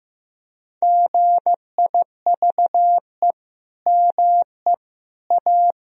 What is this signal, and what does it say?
Morse code, heard as a keyed tone, about 16 wpm, 700 Hz. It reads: GIVE ME A
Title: Sentences from Top 1000 Words - Review Part 1 - 15wpm